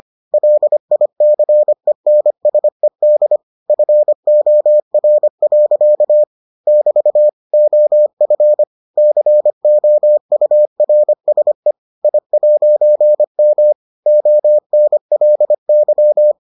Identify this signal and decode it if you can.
Morse code, heard as a keyed tone, about 25 wpm, 600 Hz. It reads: LICENSED FOR. <BT> OF COURSE I'M ONLY